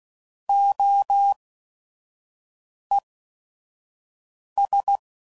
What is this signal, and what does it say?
Morse code, heard as a keyed tone, about 16 wpm, spaced as for 5 wpm, 800 Hz. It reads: OES
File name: CW Code Groups @5WPM / Technical Subjects / SwPwrSup